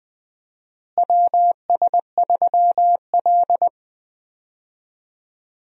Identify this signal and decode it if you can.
Morse code, heard as a keyed tone, about 20 wpm, 700 Hz. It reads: WS3L
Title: USA Call Signs - 4 Characters - Repeated 2x - 20wpm